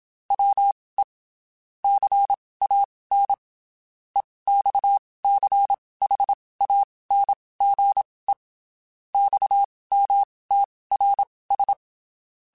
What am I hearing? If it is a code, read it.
Morse code, heard as a keyed tone, about 27 wpm, spaced as for 19 wpm, 800 Hz. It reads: WE CAN EXCHANGE XMTRS
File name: (spaced) / QSOs Spaced-mp3s / WA7QOX 1 @25 sp